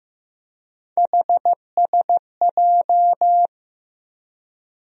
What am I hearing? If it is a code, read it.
Morse code, heard as a keyed tone, about 15 wpm, 700 Hz. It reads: HSJ
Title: USA Call Sign Postfixes - 3 Letters 15wpm